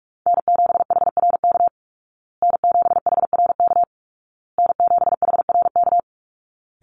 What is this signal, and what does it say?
Morse code, heard as a keyed tone, about 44 wpm, 700 Hz. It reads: N7HRK N7HRK N7HRK
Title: Call Signs Repeated 2x - 45wpm